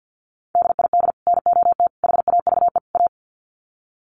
Morse code, heard as a keyed tone, about 50 wpm, 700 Hz. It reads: DID NOT HAVE A
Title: Sets of 4 Words 50wpm